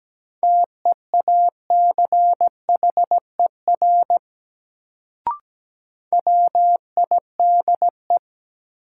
Morse code, heard as a keyed tone, about 17 wpm, 700 Hz. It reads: TEACHER WIDE
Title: Sentences from Top 900 Words - New Word Drill - 17wpm